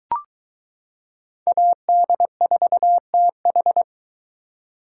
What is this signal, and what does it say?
Morse code, heard as a keyed tone, about 23 wpm, 700 Hz. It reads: AD4TH